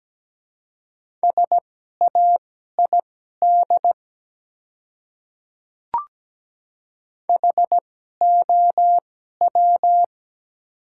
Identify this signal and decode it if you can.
Morse code, heard as a keyed tone, about 17 wpm, spaced as for 12 wpm, 700 Hz. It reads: SAID HOW
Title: 2x Character Spacing - Top 100 Words - 17wpm